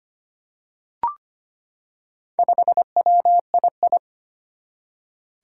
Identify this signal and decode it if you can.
Morse code, heard as a keyed tone, about 25 wpm, 700 Hz. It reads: 5WII
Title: Four Challenging Letter-Number Combinations 25wpm